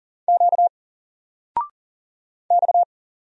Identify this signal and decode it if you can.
Morse code, heard as a keyed tone, about 40 wpm, 700 Hz. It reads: Q X
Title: Single Letter-Number 40wpm